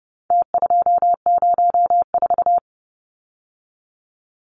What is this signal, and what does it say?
Morse code, heard as a keyed tone, about 29 wpm, 700 Hz. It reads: T204
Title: Numbers - Contest Style - 4-Digit 30wpm